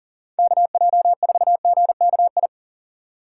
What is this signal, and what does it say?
Morse code, heard as a keyed tone, about 40 wpm, 700 Hz. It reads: KJ4GKI
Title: USA Call Signs - 6 Characters - 2x Character Spacing - 40wpm